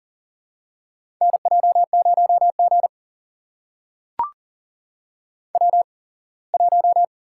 Morse code, heard as a keyed tone, about 40 wpm, spaced as for 24 wpm, 700 Hz. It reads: NJ0G W1
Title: USA Call Signs - 4 Characters - 8x Character Spacing - 40wpm